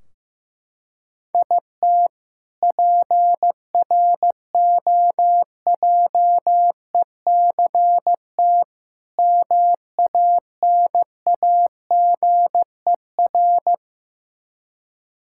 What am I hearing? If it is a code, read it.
Morse code, heard as a keyed tone, about 15 wpm, 700 Hz. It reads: IT PROJECT MANAGER
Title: K1USN SST QSO Practice 15wpm